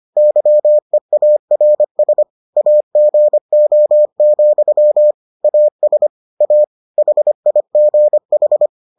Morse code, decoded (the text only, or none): YEARS AGO, AS A HIGH